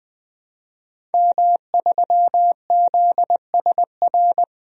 Morse code, heard as a keyed tone, about 20 wpm, 700 Hz. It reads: M3ZSR
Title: DXCC-100 Call Signs - 5 Characters - 20wpm